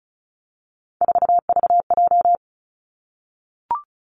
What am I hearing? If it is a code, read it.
Morse code, heard as a keyed tone, about 36 wpm, 700 Hz. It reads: 4VJ
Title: Three Challenging Letter-Number Combinations 35wpm